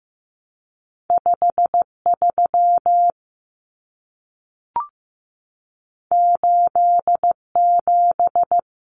Morse code, heard as a keyed tone, about 15 wpm, 700 Hz. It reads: 53 87
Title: Common Cut Numbers - 2-Digit 15wpm